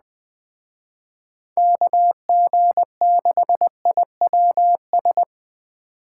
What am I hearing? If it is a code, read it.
Morse code, heard as a keyed tone, about 21 wpm, 700 Hz. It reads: KG6IWS